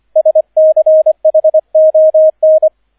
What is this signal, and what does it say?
Morse code, heard as a keyed tone, about 25 wpm, 630 Hz, with radio noise. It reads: SCHON
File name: 003 Dritter Streich 25wpm